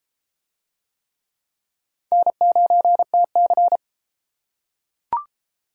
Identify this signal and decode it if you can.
Morse code, heard as a keyed tone, about 29 wpm, 700 Hz. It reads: N9TC